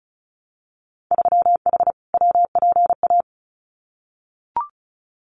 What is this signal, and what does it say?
Morse code, heard as a keyed tone, about 36 wpm, 700 Hz. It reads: 3H WPA
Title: Winter Field Day Exchanges 35wpm